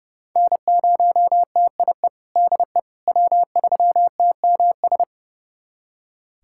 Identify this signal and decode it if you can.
Morse code, heard as a keyed tone, about 32 wpm, 700 Hz. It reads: N0TIE DE W3TMS
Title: USA Callsign DE Callsign - 5 Characters - 30wpm